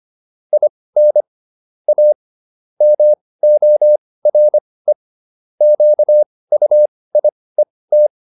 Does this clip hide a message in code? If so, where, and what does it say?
Yes: Morse code from 0.5 to 8.1 s, heard as a keyed tone, about 25 wpm, spaced as for 18 wpm, 600 Hz. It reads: IN A MORE QUIET